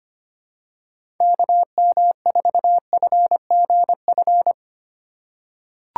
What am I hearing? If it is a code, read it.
Morse code, heard as a keyed tone, about 25 wpm, 700 Hz. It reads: KM4FGF
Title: USA Call Signs - 6 Characters - 4x Character Spacing - 25wpm